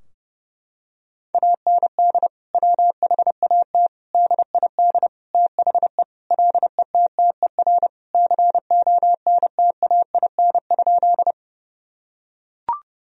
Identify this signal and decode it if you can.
Morse code, heard as a keyed tone, about 30 wpm, 700 Hz. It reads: AND WHAT DID THE LETTER CONTAIN?
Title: Sentences from Top 1000 Words - Review Part 2 - 30wpm